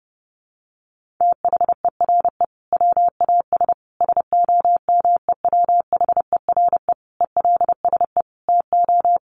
Morse code, heard as a keyed tone, about 30 wpm, 700 Hz. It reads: THERE WAS SOMEWHERE ELSE TO
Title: Sentences from Top 1400 Words 30wpm